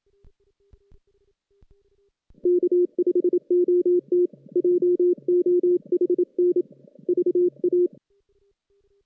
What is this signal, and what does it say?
Morse code, heard as a keyed tone, about 27 wpm, 380 Hz. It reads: K5OT JOHN VA
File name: 2016 JAN CW NAQP